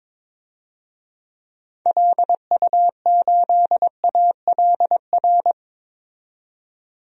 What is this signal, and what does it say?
Morse code, heard as a keyed tone, about 22 wpm, 700 Hz. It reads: LU8ALR
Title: DXCC-100 Call Signs - 6 Characters - 22wpm